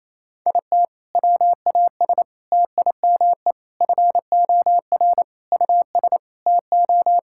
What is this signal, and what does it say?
Morse code, heard as a keyed tone, about 28 wpm, 700 Hz. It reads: IT WAS TIME FOR US TO
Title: Sentences from Top 1600 Words 28wpm